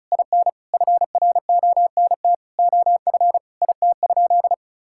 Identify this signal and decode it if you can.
Morse code, heard as a keyed tone, about 36 wpm, 700 Hz. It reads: IN FRONT OF IT?
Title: Sentences from Top 1000 Words - Review Part 5 - 35wpm